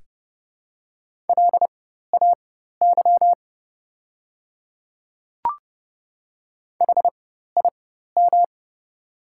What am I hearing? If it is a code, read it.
Morse code, heard as a keyed tone, about 30 wpm, spaced as for 14 wpm, 690 Hz. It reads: LAY HIM
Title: 4x Character Spacing - Top 500 Words - 30wpm